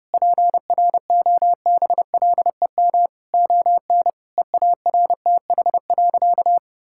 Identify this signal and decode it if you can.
Morse code, heard as a keyed tone, about 32 wpm, 700 Hz. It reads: PROBLEM ON EARTH.
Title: Sentences from Top 1000 Words - Review Part 6 - 30wpm